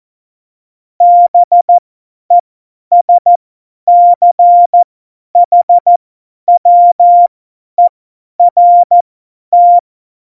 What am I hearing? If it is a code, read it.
Morse code, heard as a keyed tone, about 14 wpm, spaced as for 10 wpm, 700 Hz. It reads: BESCHWERT
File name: Morsekurs / Witze / Witz 03.Dezember 2019, 50 F70